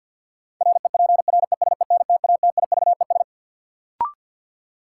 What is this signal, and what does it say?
Morse code, heard as a keyed tone, about 52 wpm, 700 Hz. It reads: REPRESENTATIVES
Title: Sets of 5 Words Encore - New Word Drill - 50wpm